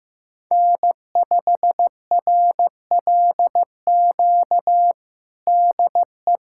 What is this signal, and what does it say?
Morse code, heard as a keyed tone, about 15 wpm, 700 Hz. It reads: N5RLQ DE
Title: USA Callsign DE Callsign - 5 Characters - 15wpm